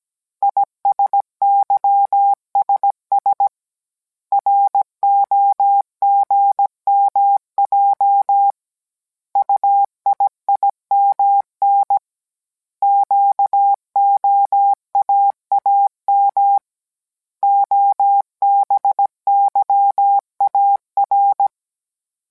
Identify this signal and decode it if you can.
Morse code, heard as a keyed tone, about 17 wpm, 800 Hz. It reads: ISYSS ROGMJ UIIMN QOAAM OBYAR